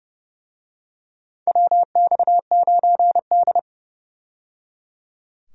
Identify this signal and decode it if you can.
Morse code, heard as a keyed tone, about 31 wpm, 700 Hz. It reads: WX9D
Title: USA Call Signs - 4 Characters - 30wpm